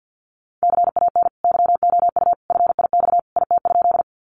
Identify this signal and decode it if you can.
Morse code, heard as a keyed tone, about 50 wpm, 700 Hz. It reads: CAN YOU FIX IT?
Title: Sentences from Top 1400 Words 50wpm